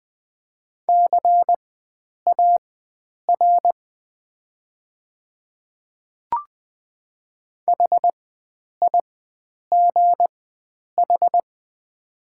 Morse code, heard as a keyed tone, about 20 wpm, spaced as for 9 wpm, 700 Hz. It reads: CAR HIGH
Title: 4x Character Spacing - Top 300 Words - 20wpm